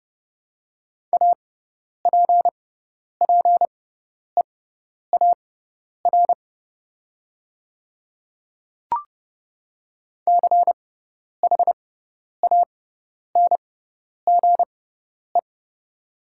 Morse code, heard as a keyed tone, about 30 wpm, spaced as for 10 wpm, 690 Hz. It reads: APPEAR CHANGE
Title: 6x Character Spacing - Top 500 Words - 30wpm